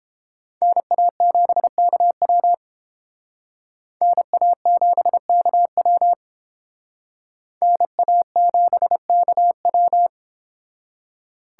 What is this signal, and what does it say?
Morse code, heard as a keyed tone, about 31 wpm, 700 Hz. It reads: NA7KW NA7KW NA7KW